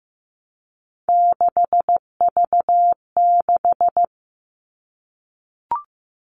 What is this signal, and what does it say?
Morse code, heard as a keyed tone, about 15 wpm, 700 Hz. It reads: 6V6
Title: All Cut Numbers - 3-Digit 15wpm